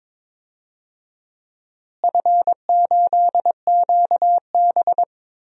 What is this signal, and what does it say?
Morse code, heard as a keyed tone, about 22 wpm, 700 Hz. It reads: F8QB
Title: DXCC-100 Call Signs - 4 Characters - Repeated 3x - 22wpm